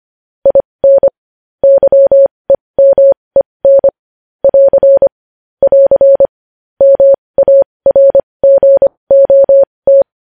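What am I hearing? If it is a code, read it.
Morse code, heard as a keyed tone, about 25 wpm, spaced as for 20 wpm, 550 Hz. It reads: IN YEMEN <AR> <AR> MARGOT
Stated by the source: Morse code news headlines practice at 20 wpm.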